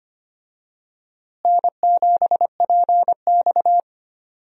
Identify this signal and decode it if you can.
Morse code, heard as a keyed tone, about 25 wpm, 700 Hz. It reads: N7PX
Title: USA Call Signs - 4 Characters - Repeated 2x - 25wpm